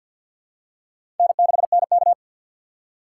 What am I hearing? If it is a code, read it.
Morse code, heard as a keyed tone, about 52 wpm, 700 Hz. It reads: N6NK
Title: USA Call Signs - 3x Character Spacing - 50wpm